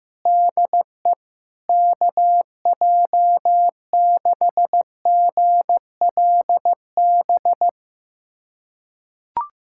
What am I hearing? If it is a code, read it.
Morse code, heard as a keyed tone, about 15 wpm, 700 Hz. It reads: DE KJ6GLB
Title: USA Callsign DE Callsign - 6 Characters - 15wpm